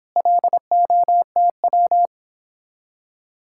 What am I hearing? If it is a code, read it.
Morse code, heard as a keyed tone, about 29 wpm, 700 Hz. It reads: LOTW